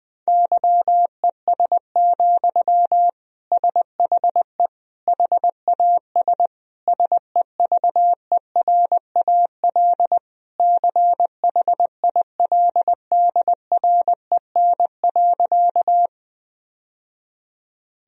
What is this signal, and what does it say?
Morse code, heard as a keyed tone, about 20 wpm, 700 Hz. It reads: YES, SHE HAS SEVERAL CHILDREN.
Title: Sentences from Top 1000 Words - Review Part 2 - 20wpm